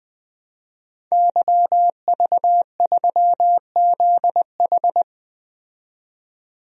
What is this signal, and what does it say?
Morse code, heard as a keyed tone, about 20 wpm, 700 Hz. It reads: YV3ZH
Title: DXCC-100 Call Signs - 5 Characters - 20wpm